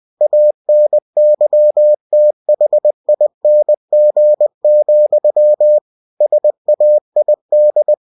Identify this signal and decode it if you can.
Morse code, heard as a keyed tone, about 20 wpm, 600 Hz. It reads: ANYTHING, SAID
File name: cw-books / winnie-the-pooh / winnie-20wpm-page-0089